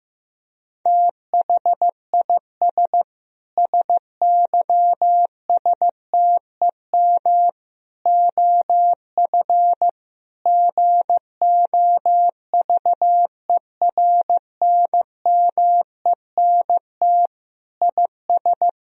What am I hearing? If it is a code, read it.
Morse code, heard as a keyed tone, about 16 wpm, 700 Hz. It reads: THIS SYSTEM OF GOVERNMENT IS
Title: Sentences from Top 1000 Words - Review Part 3 - 15wpm